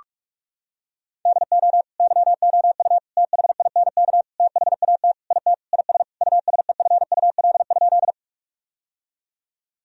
Morse code, heard as a keyed tone, about 46 wpm, 700 Hz. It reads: DO YOU THINK THAT IT IS USEFUL?
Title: Sentences from Top 1000 Words - Review Part 4 - 45wpm